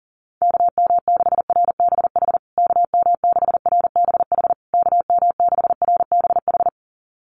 Morse code, heard as a keyed tone, about 37 wpm, 700 Hz. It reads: KM6RBH KM6RBH KM6RBH
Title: USA Call Signs Repeated 3x 40wpm